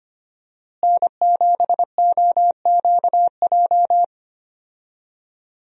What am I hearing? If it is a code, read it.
Morse code, heard as a keyed tone, about 25 wpm, 700 Hz. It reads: N7OQJ
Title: USA Call Signs - 5 Characters - Repeated 3x - 25wpm